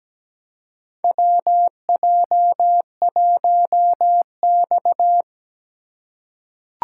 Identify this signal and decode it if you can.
Morse code, heard as a keyed tone, about 17 wpm, 700 Hz. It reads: WJ1X
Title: USA Call Signs - 4 Characters - 2x Character Spacing - 17wpm